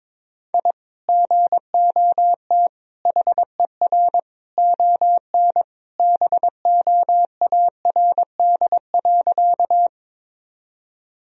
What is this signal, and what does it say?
Morse code, heard as a keyed tone, about 21 wpm, 700 Hz. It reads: I GOT HER ON BOARD.
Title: Sentences from Top 1000 Words - Review Part 4 - 22wpm